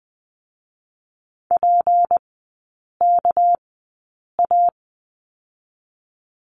Morse code, heard as a keyed tone, about 20 wpm, spaced as for 8 wpm, 700 Hz. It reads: PKA
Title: 205 Lesson 33e - Letter j - Three Characters 20wpm